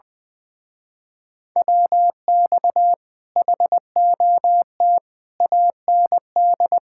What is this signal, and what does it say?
Morse code, heard as a keyed tone, about 21 wpm, 700 Hz. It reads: WX HOT AND
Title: Speed-Racing Common QSO Phrases Encore - 20wpm